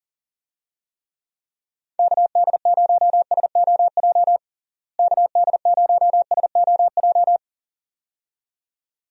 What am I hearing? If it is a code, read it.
Morse code, heard as a keyed tone, about 40 wpm, 700 Hz. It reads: KD0SOJ KD0SOJ
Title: USA Call Signs - 6 Characters - Repeated 2x - 40wpm